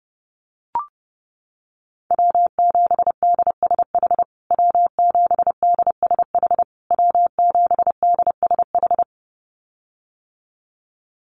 Morse code, heard as a keyed tone, about 30 wpm, 700 Hz. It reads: W7DSH W7DSH W7DSH
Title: USA Call Signs Repeated 3x 30wpm